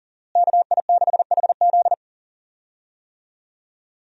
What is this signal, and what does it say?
Morse code, heard as a keyed tone, about 40 wpm, 700 Hz. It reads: KI6HZ
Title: USA Call Signs - 5 Characters - Repeated 3x - 40wpm